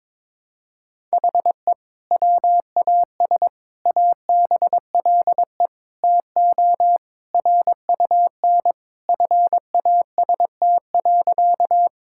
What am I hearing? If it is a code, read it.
Morse code, heard as a keyed tone, about 22 wpm, 700 Hz. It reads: HE WAS ABLE TO RUN FAST.
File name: practice / 22wpm / Sentences with 2-Word Phrases - 22wpm